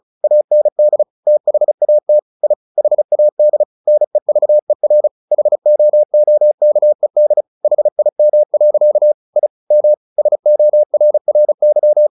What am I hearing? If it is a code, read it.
Morse code, heard as a keyed tone, about 34 wpm, 600 Hz. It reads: AND THAT I HAD NEVER HOOKED HIM. I M SORRY
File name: the old man and the sea 2 35wpm